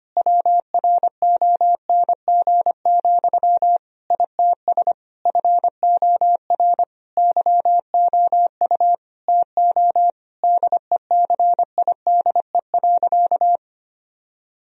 Morse code, heard as a keyed tone, about 25 wpm, 700 Hz. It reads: WRONG, ITS FOR YOU TO DECIDE.
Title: Sentences from Top 1000 Words - Review Part 3 - 25wpm